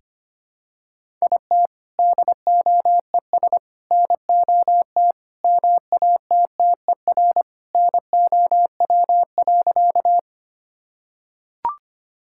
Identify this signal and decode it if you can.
Morse code, heard as a keyed tone, about 25 wpm, 700 Hz. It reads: IT DOES NOT MATTER NOW.
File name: practice / 25wpm / Sentences with 4-Word Phrases - 25wpm